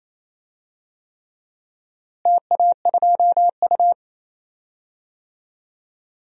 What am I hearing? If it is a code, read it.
Morse code, heard as a keyed tone, about 28 wpm, 700 Hz. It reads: TA2U
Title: DXCC-100 Call Signs - 4 Characters - Repeated 3x - 28wpm